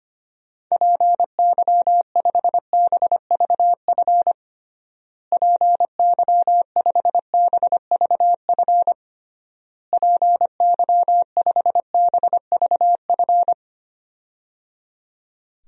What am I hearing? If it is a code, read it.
Morse code, heard as a keyed tone, about 25 wpm, 700 Hz. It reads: PY5BVF PY5BVF PY5BVF
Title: DXCC-100 Call Signs - 6 Characters - Repeated 3x - 25wpm